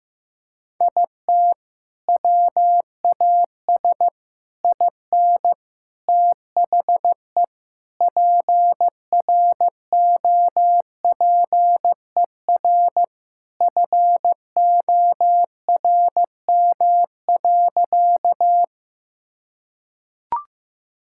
Morse code, heard as a keyed tone, about 15 wpm, 700 Hz. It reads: IT WAS IN THE PROPER FORM.
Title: Sentences from Top 1000 Words - Review Part 5 - 15wpm